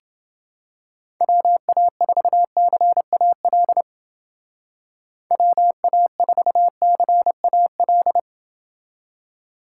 Morse code, heard as a keyed tone, about 29 wpm, 700 Hz. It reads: WA4CAL WA4CAL